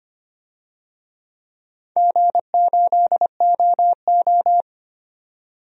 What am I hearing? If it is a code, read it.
Morse code, heard as a keyed tone, about 25 wpm, 700 Hz. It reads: G8OO
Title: DXCC-100 Call Signs - 4 Characters - 25wpm